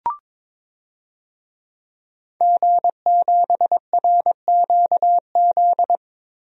Morse code, heard as a keyed tone, about 22 wpm, 700 Hz. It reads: G7RQZ
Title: DXCC-100 Call Signs - 5 Characters - Repeated 2x - 22wpm